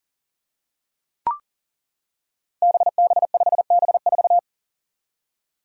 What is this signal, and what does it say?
Morse code, heard as a keyed tone, about 39 wpm, 700 Hz. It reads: BB5B4
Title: Five Challenging Letter-Number Combinations 40wpm